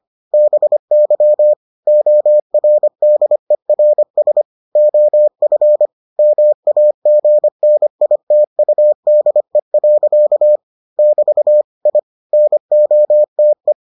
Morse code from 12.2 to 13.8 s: NOTE

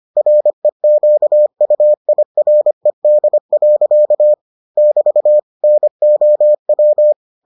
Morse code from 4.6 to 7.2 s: <BT> NOW

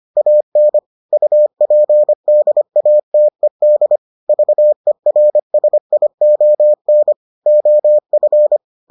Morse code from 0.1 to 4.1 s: AN UPDATED

